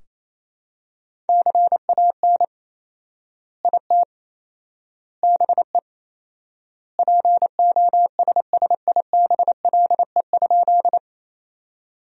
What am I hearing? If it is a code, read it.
Morse code, heard as a keyed tone, about 28 wpm, 700 Hz. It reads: CAN IT BE POSSIBLE?
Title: 4x Word-Spacing - Sentences from Top 500 Words - 28wpm